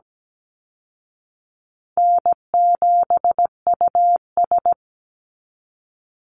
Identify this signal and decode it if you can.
Morse code, heard as a keyed tone, about 17 wpm, 700 Hz. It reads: N7US
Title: Call Signs Repeated 3x - 17wpm